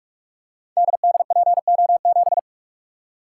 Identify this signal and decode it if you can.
Morse code, heard as a keyed tone, about 44 wpm, 700 Hz. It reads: DDPO7
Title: Code Groups - 5 Letter:Numbers - 45wpm